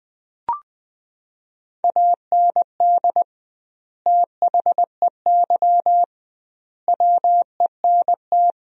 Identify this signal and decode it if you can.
Morse code, heard as a keyed tone, about 21 wpm, 700 Hz. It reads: AND THEY WENT
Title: 2x Word-Spacing - Sentences from Top 200 Words - 20wpm